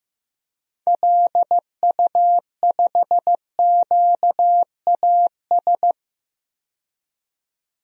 Morse code, heard as a keyed tone, about 15 wpm, 700 Hz. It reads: LU5QAS
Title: DXCC-100 Call Signs - 6 Characters - 15wpm